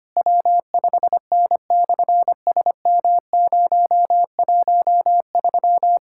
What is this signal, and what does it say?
Morse code, heard as a keyed tone, about 24 wpm, 700 Hz. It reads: W5N/SM013
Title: SOTA QSO Practice 25wpm